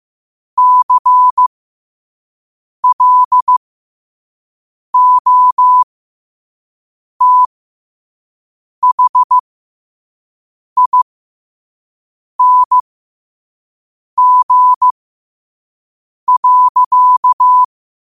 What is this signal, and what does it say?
Morse code, heard as a keyed tone, about 15 wpm, spaced as for 5 wpm, 1000 Hz. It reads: CLOTHING.
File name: qotdfiles / 5wpm